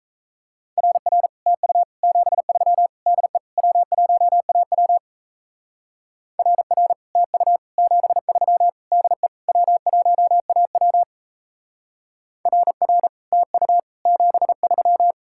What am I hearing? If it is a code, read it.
Morse code, heard as a keyed tone, about 36 wpm, 700 Hz. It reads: RR TU 73 DE W1AW RR TU 73 DE W1AW RR TU 73